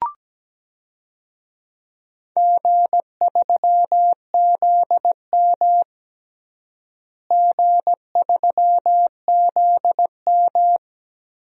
Morse code, heard as a keyed tone, about 17 wpm, 700 Hz. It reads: G3ZM G3ZM
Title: DXCC-100 Call Signs - 4 Characters - Repeated 3x - 17wpm